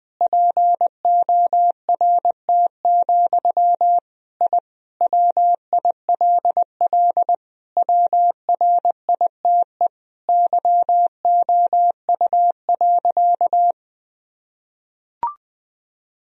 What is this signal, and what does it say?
Morse code, heard as a keyed tone, about 20 wpm, 700 Hz. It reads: PORT, I WILL WRITE YOU.
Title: Sentences from Top 1000 Words - Review Part 1 - 20wpm